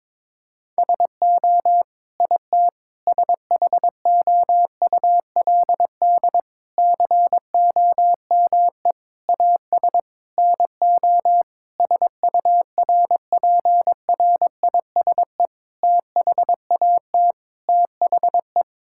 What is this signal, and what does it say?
Morse code, heard as a keyed tone, about 22 wpm, 700 Hz. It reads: SO IT SHOULD COME AS NO SURPRISE THAT THE
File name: Sets of 9 Words - 22wpm